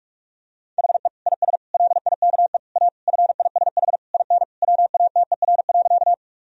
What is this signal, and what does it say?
Morse code, heard as a keyed tone, about 46 wpm, 700 Hz. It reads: HE IS LIKE A FISH IN WATER.
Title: Sentences from Top 1000 Words - Review Part 1 - 45wpm